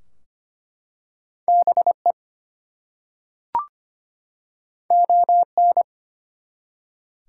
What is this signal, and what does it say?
Morse code, heard as a keyed tone, about 25 wpm, 700 Hz. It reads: BE ON
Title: Two Letter Words - 25wpm